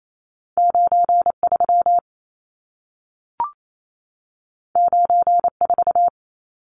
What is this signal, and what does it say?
Morse code, heard as a keyed tone, about 28 wpm, 700 Hz. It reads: 93 94
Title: Numbers - 2-Digit 28wpm